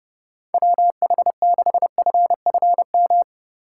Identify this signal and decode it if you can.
Morse code, heard as a keyed tone, about 29 wpm, 700 Hz. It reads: WH6FFM
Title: USA Call Signs - 6 Characters - 8x Character Spacing - 30wpm